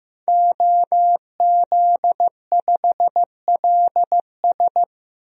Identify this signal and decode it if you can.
Morse code, heard as a keyed tone, about 15 wpm, 700 Hz. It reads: OZ5LS
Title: DXCC-100 Call Signs - 5 Characters - 15wpm